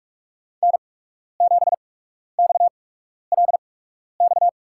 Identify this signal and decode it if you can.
Morse code, heard as a keyed tone, about 44 wpm, spaced as for 12 wpm, 690 Hz. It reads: N7XLX
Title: USA Call Signs - 5 Characters - 8x Character Spacing - 45wpm